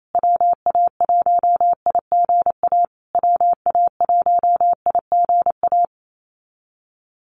Morse code, heard as a keyed tone, about 28 wpm, 700 Hz. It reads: WA1IGA WA1IGA
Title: USA Call Signs Repeated 3x 28wpm